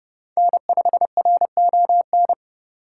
Morse code, heard as a keyed tone, about 31 wpm, 700 Hz. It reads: N5RON